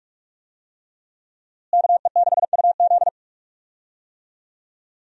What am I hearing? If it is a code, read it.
Morse code, heard as a keyed tone, about 48 wpm, 700 Hz. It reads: KE6UZ
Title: USA Call Signs - 5 Characters - Repeated 3x - 45wpm